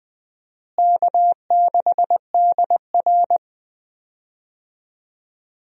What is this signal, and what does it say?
Morse code, heard as a keyed tone, about 20 wpm, 700 Hz. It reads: K6DR
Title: USA Call Signs - 4 Characters - Repeated 2x - 20wpm